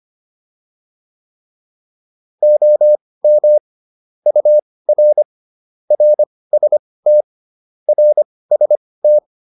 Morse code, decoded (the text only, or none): OM UR RST RST